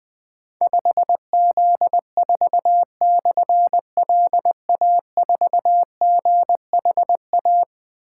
Morse code, heard as a keyed tone, about 21 wpm, 700 Hz. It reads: 5Z4/LA4GHA
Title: Difficult Call Signs 20wpm